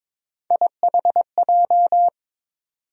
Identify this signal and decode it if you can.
Morse code, heard as a keyed tone, about 22 wpm, 700 Hz. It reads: IHJ
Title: USA Call Sign Postfixes - 3 Letters 22wpm